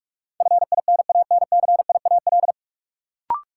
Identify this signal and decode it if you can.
Morse code, heard as a keyed tone, about 47 wpm, 700 Hz. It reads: FINANCIAL
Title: Sentences from Top 1000 Words - Review Part 5 - New Word Drill - 45wpm